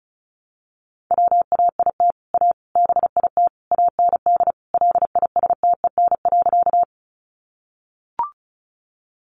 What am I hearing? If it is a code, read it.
Morse code, heard as a keyed tone, about 34 wpm, 700 Hz. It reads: WAIT A BIT AND LISTEN.
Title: Sentences from Top 900 Words 35wpm